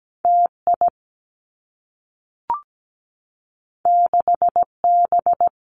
Morse code, heard as a keyed tone, about 17 wpm, 700 Hz. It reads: T I 6 B
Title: Two Challenging Letter-Number Combinations 17wpm